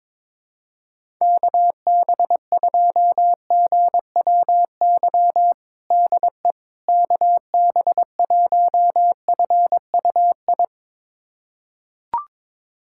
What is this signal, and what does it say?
Morse code, heard as a keyed tone, about 22 wpm, 700 Hz. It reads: KB2GWY DE KB1FUI
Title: USA Callsign DE Callsign - 6 Characters - 22wpm